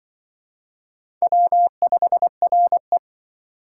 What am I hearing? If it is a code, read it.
Morse code, heard as a keyed tone, about 24 wpm, 700 Hz. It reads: W5RE
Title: Speed Racing - USA 1x2 Call Signs 20wpm